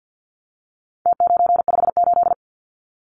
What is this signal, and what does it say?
Morse code, heard as a keyed tone, about 50 wpm, 700 Hz. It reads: T958
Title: Numbers - Contest Style - 4-Digit 50wpm